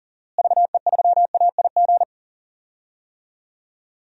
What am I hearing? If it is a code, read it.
Morse code, heard as a keyed tone, about 40 wpm, 700 Hz. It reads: VE3AIG
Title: DXCC-100 Call Signs - 6 Characters - Repeated 3x - 40wpm